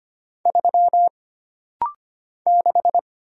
Morse code, heard as a keyed tone, about 25 wpm, 700 Hz. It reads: 3 6
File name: Single Character (Letter:Number:Punctuation:Prosign) - Rapid-Fire - 25wpm